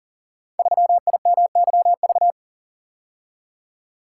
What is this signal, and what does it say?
Morse code, heard as a keyed tone, about 42 wpm, 700 Hz. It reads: 3IMYV
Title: Code Groups - 5 Letter:Numbers - 40wpm